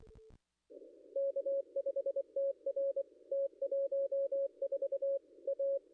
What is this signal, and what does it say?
Morse code, heard as a keyed tone, about 23 wpm, 550 Hz. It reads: K5TR T 14 A
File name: contests / ss-cw-k5tr-op-n5rz -0100